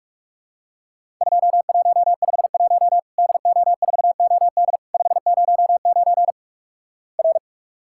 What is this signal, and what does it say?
Morse code, heard as a keyed tone, about 48 wpm, 700 Hz. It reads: 2151 DO4OD 509 R
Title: Worked All Europe DX Contest - QTC Exchange - 45wpm